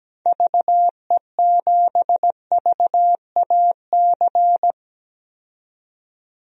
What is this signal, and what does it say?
Morse code, heard as a keyed tone, about 17 wpm, 700 Hz. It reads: VE7VAC
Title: DXCC-100 Call Signs - 6 Characters - 17wpm